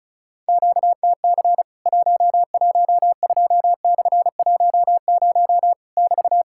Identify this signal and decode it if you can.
Morse code, heard as a keyed tone, about 36 wpm, 700 Hz. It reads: QTC 112/10 <BT>
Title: Worked All Europe DX Contest - QTC Exchange - 35wpm